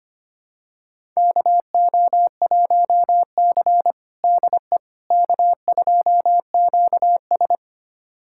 Morse code, heard as a keyed tone, about 25 wpm, 700 Hz. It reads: KO1C DE K2QS
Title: USA Callsign DE Callsign - 4 Characters - 25wpm